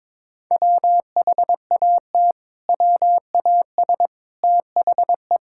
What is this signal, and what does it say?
Morse code, heard as a keyed tone, about 21 wpm, 700 Hz. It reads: WHAT WAS THE
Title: Sentences from Top 1600 Words 22wpm